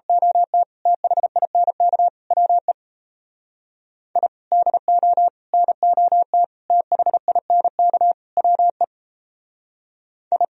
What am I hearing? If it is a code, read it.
Morse code, heard as a keyed tone, about 32 wpm, 700 Hz. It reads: OT THINK WE I DO NOT THINK WE I
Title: Speed-Racing - Sets of 5 Words 28wpm